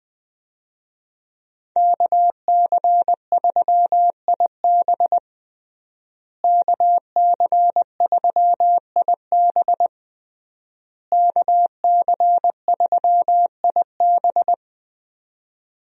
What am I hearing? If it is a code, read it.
Morse code, heard as a keyed tone, about 20 wpm, 700 Hz. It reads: KC3IB KC3IB KC3IB
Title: USA Call Signs - 5 Characters - Repeated 3x - 20wpm